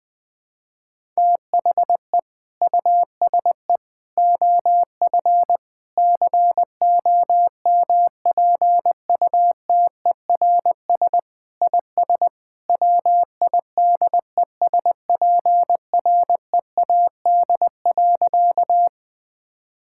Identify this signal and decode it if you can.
Morse code, heard as a keyed tone, about 20 wpm, 700 Hz. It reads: THE USE OF COMPUTERS IS WIDESPREAD.
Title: Sentences with 3-Word Phrases - Spoken Beginning - 20wpm